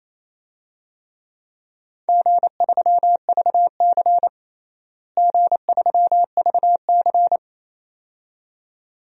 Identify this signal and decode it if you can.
Morse code, heard as a keyed tone, about 28 wpm, 700 Hz. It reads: G3VC G3VC
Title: DXCC-100 Call Signs - 4 Characters - Repeated 2x - 28wpm